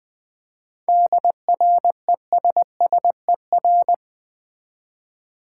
Morse code, heard as a keyed tone, about 20 wpm, 700 Hz. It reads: DRESSER
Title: Easy Words with Hints - 20wpm